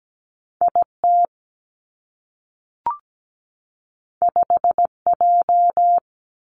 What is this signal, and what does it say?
Morse code, heard as a keyed tone, about 17 wpm, 700 Hz. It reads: I T 5 J
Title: Two Challenging Letter-Number Combinations 17wpm